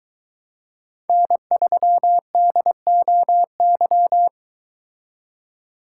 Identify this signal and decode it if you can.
Morse code, heard as a keyed tone, about 22 wpm, 700 Hz. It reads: N3DOY